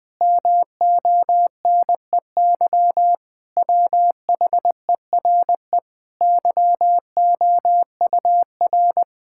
Morse code, heard as a keyed tone, about 20 wpm, 700 Hz. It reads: MONEY WHERE YOUR
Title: Sets of 8 Words - 20wpm